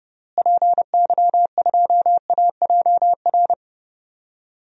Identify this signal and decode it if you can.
Morse code, heard as a keyed tone, about 32 wpm, 700 Hz. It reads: PY2AJR
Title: DXCC-100 Call Signs - 6 Characters - Repeated 3x - 30wpm